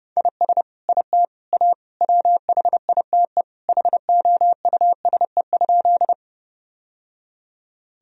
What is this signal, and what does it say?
Morse code, heard as a keyed tone, about 30 wpm, 700 Hz. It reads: IS IT A WHITE HOUSE?
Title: Sentences from Top 1000 Words - Review Part 1 - 30wpm